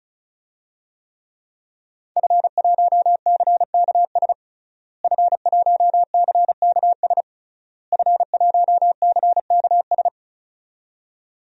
Morse code, heard as a keyed tone, about 34 wpm, 700 Hz. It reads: F1CKS F1CKS F1CKS
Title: DXCC-100 Call Signs - 5 Characters - Repeated 3x - 35wpm